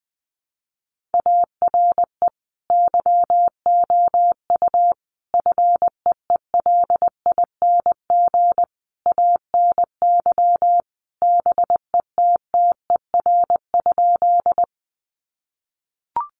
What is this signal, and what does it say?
Morse code, heard as a keyed tone, about 19 wpm, 700 Hz. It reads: ARE YOU FEELING ANY BETTER?
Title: Sentences from Top 1200 Words 20wpm